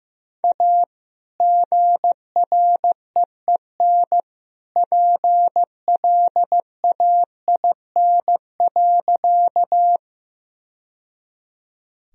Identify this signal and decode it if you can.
Morse code, heard as a keyed tone, about 15 wpm, 700 Hz. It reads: A GREEN PLAIN.
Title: Sentences from Top 1000 Words - Review Part 2 - 15wpm